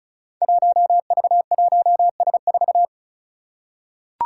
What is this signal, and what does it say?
Morse code, heard as a keyed tone, about 36 wpm, 700 Hz. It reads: 1V1S4
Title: Five Challenging Letter-Number Combinations 35wpm